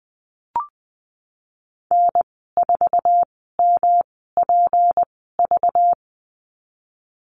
Morse code, heard as a keyed tone, about 20 wpm, spaced as for 15 wpm, 700 Hz. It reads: N4MPV
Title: 193 Lesson 31g - Letter v - Call Signs 20wpm